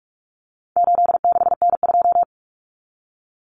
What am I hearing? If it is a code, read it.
Morse code, heard as a keyed tone, about 44 wpm, 700 Hz. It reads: 86N2
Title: Common Cut Numbers - 4-Digit 45wpm